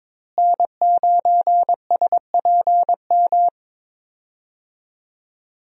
Morse code, heard as a keyed tone, about 22 wpm, 700 Hz. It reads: N9SPM
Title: USA Call Signs - 5 Characters - Repeated 2x - 22wpm